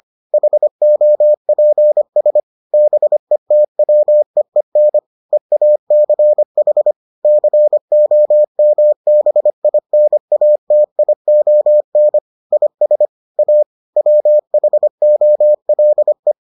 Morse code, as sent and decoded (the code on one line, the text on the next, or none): .... --- .--. ... / -... - .-- . -. / . .- -.-. .... / -.-. --- -- -... .. -. .- - .. --- -. / .. ... / .- / .-- .... --- .-.. .
HOPS BETWEEN EACH COMBINATION IS A WHOLE